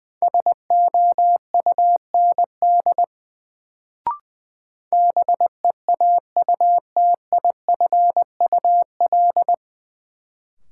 Morse code, heard as a keyed tone, about 20 wpm, 700 Hz. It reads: SOUND BEAUTIFUL
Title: Sentences from Top 900 Words - New Word Drill - 20wpm